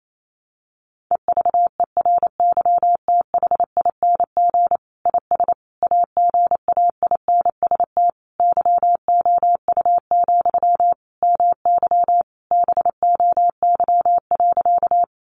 Morse code, heard as a keyed tone, about 28 wpm, 700 Hz. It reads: EVERYTHING IS AGAINST YOU, MY BOY.
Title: Sentences from Top 700 Words 28wpm